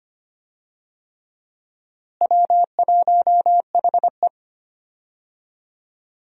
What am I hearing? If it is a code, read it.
Morse code, heard as a keyed tone, about 25 wpm, 700 Hz. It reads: W1HE
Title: USA Call Signs - 4 Characters - Repeated 3x - 25wpm